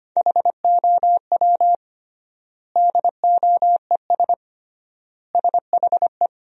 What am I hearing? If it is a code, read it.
Morse code, heard as a keyed tone, about 25 wpm, 700 Hz. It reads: HOW DOES SHE
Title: 3x Word-Spacing - Sentences from Top 200 Words - 25wpm